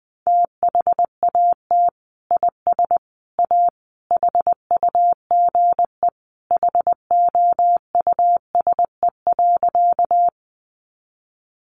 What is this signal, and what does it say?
Morse code, heard as a keyed tone, about 20 wpm, 700 Hz. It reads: THAT IS A HUGE HOUSE.
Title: Sentences from Top 1100 Words 20wpm